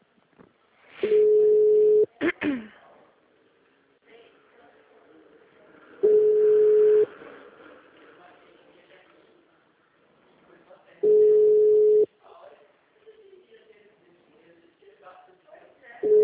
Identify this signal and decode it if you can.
Ringback tone (Europe, 425 Hz), with speech.